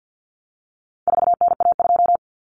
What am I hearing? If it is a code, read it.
Morse code, heard as a keyed tone, about 50 wpm, 700 Hz. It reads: 4NA2
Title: Common Cut Numbers - 4-Digit 50wpm